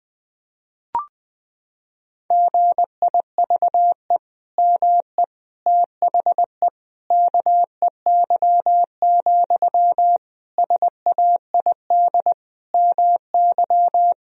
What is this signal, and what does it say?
Morse code, heard as a keyed tone, about 20 wpm, 700 Hz. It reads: GIVE ME THE KEY, SAID MY
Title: Sentences from Top 1000 Words - Review Part 4 - 20wpm